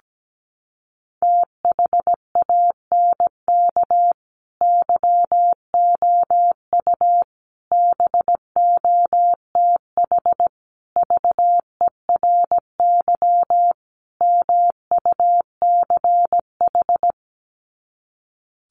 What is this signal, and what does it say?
Morse code, heard as a keyed tone, about 17 wpm, 700 Hz. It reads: THANK YOU BOTH VERY MUCH
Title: Sets of 5 Words 17wpm